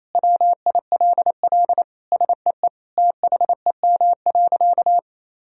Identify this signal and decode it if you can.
Morse code, heard as a keyed tone, about 27 wpm, 700 Hz. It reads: WILL SEE THEM.
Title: Sentences from Top 1000 Words - Review Part 2 - 28wpm